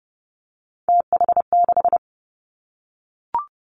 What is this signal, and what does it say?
Morse code, heard as a keyed tone, about 30 wpm, 700 Hz. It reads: TH6
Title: Three Challenging Letter-Number Combinations 30wpm